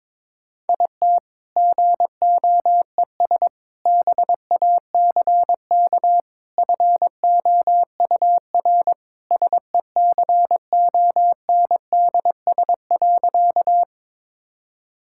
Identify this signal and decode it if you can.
Morse code, heard as a keyed tone, about 22 wpm, 690 Hz. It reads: IT GOES BACK FOUR SECONDS.
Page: morse-code-ninja-practice-22wpm